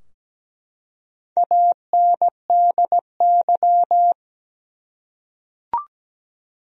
Morse code, heard as a keyed tone, about 17 wpm, 700 Hz. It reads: ANDY
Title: USA Common Names - Max 4 Letters - 17wpm